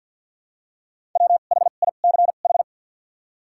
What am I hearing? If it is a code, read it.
Morse code, heard as a keyed tone, about 52 wpm, spaced as for 36 wpm, 700 Hz. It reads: WHICH
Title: 2x Character Spacing - Top 300 Words - 50wpm